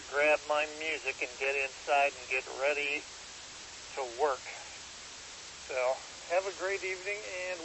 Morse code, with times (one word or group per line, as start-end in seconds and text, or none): none